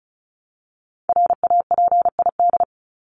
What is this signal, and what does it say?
Morse code, heard as a keyed tone, about 34 wpm, 700 Hz. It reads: RAPID
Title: LY Suffix - Top 100 Words with Base Words 35wpm